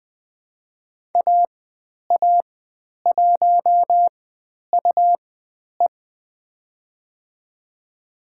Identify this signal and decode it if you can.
Morse code, heard as a keyed tone, about 20 wpm, spaced as for 10 wpm, 700 Hz. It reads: AA1UE
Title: 109 Lesson 18g - Letter m - Call Signs 20wpm